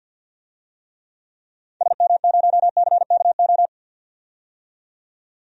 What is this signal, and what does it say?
Morse code, heard as a keyed tone, about 51 wpm, 700 Hz. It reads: SM0CKO
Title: DXCC-100 Call Signs - 6 Characters - 50wpm